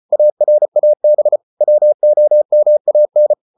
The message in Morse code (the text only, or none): ARAB WOMAN